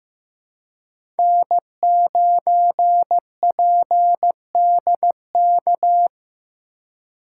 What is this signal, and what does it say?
Morse code, heard as a keyed tone, about 15 wpm, 700 Hz. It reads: N9PDK
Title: USA Call Signs - 5 Characters - 4x Character Spacing - 15wpm